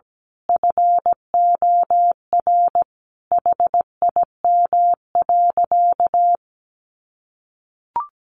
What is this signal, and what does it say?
Morse code, heard as a keyed tone, about 17 wpm, 700 Hz. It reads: FOR HIM.
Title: Sentences from Top 1400 Words 17wpm